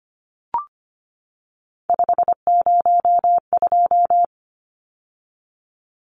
Morse code, heard as a keyed tone, about 25 wpm, 700 Hz. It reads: 502
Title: Numbers - 3-Digit 25wpm